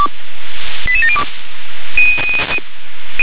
Five-tone selective call (ZVEI), heard as quick runs of five tones, tones of about 70 ms. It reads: 88702